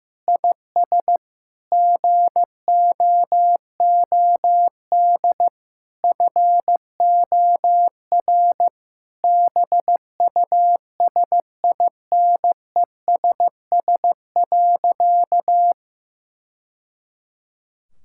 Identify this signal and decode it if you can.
Morse code, heard as a keyed tone, about 16 wpm, 700 Hz. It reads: IS GOOD FOR BUSINESS.
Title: Sentences from Top 1000 Words - Review Part 6 - 15wpm